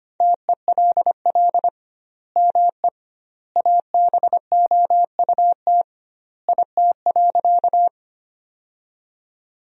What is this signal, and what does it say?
Morse code, heard as a keyed tone, about 25 wpm, 700 Hz. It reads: TELL ME ABOUT IT.
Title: 2x Word-Spacing - Sentences from Top 200 Words - 25wpm